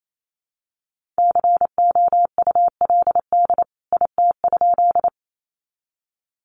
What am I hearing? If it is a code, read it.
Morse code, heard as a keyed tone, about 28 wpm, 700 Hz. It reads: COULD IT?
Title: A Sound Of Thunder 28wpm